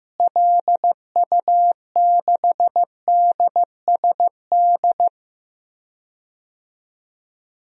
Morse code, heard as a keyed tone, about 15 wpm, 700 Hz. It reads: LU6DSD
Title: DXCC-100 Call Signs - 6 Characters - Repeated 3x - 15wpm